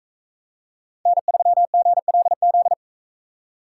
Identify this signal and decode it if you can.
Morse code, heard as a keyed tone, about 40 wpm, 700 Hz. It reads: N3GLZ